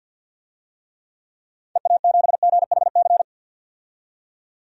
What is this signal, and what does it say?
Morse code, heard as a keyed tone, about 52 wpm, 700 Hz. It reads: EA7GHC
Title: DXCC-100 Call Signs - 6 Characters - 50wpm